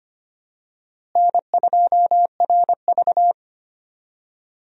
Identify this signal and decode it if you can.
Morse code, heard as a keyed tone, about 25 wpm, 700 Hz. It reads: N2RV
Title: USA Call Signs - 4 Characters - Repeated 3x - 25wpm